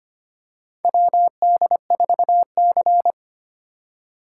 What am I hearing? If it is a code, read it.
Morse code, heard as a keyed tone, about 25 wpm, 700 Hz. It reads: WD4C
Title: USA Call Signs - 4 Characters - 4x Character Spacing - 25wpm